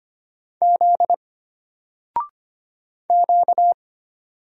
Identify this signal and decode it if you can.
Morse code, heard as a keyed tone, about 25 wpm, 700 Hz. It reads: Z Q
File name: Single-Letter-Number-25wpm